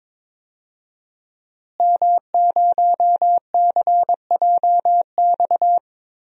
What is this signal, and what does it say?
Morse code, heard as a keyed tone, about 23 wpm, 700 Hz. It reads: M0CJX
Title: DXCC-100 Call Signs - 5 Characters - Repeated 3x - 22wpm